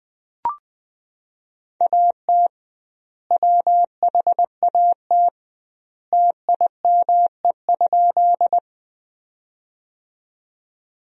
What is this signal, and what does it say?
Morse code, heard as a keyed tone, about 20 wpm, 700 Hz. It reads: AT WHAT TIME?
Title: 2x Word-Spacing - Sentences from Top 100 Words - 20wpm